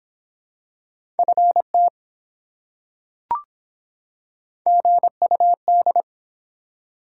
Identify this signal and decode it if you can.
Morse code, heard as a keyed tone, about 23 wpm, 700 Hz. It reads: FT GUD